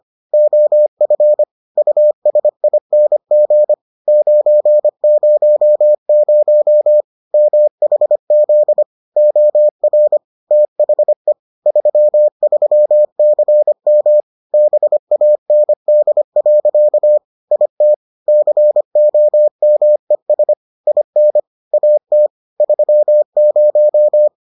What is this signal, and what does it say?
Morse code, heard as a keyed tone, about 25 wpm, 600 Hz. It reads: OF USING 900 MHZ OR THE 33CM BAND. IT COMES IN AT 30